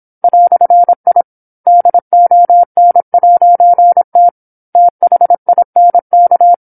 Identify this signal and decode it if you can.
Morse code, heard as a keyed tone, about 27 wpm, 700 Hz. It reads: "I DON'T THINK
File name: 08-Chapter-08 26 WPM